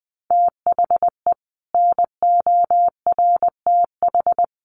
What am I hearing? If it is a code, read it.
Morse code, heard as a keyed tone, about 20 wpm, 700 Hz. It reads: THE NORTH
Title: Sets of 5 Words Encore 20wpm